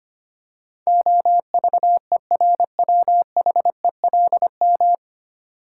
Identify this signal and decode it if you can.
Morse code, heard as a keyed tone, about 25 wpm, 700 Hz. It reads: OVERWHELM
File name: Nine Letter Words - 25wpm